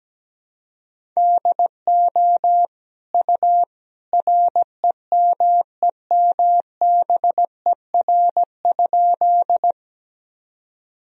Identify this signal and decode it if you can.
Morse code, heard as a keyed tone, about 17 wpm, 700 Hz. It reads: DO U REMEMBER?
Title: Sentences from Top 400 Words and Abbreviations 17wpm